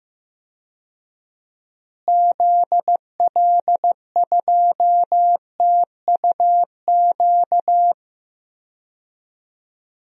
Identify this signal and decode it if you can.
Morse code, heard as a keyed tone, about 15 wpm, 700 Hz. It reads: ZL2TUQ
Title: DXCC-100 Call Signs - 6 Characters - Repeated 2x - 15wpm